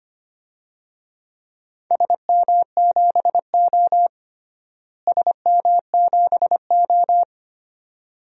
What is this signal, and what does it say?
Morse code, heard as a keyed tone, about 25 wpm, 700 Hz. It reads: SM7O SM7O
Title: DXCC-100 Call Signs - 4 Characters - Repeated 2x - 25wpm